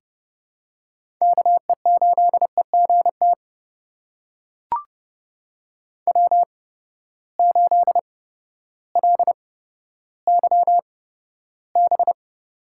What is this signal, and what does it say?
Morse code, heard as a keyed tone, about 30 wpm, spaced as for 14 wpm, 700 Hz. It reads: KE8EGT W8LYB
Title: USA Call Signs - 8x Character Spacing - 30wpm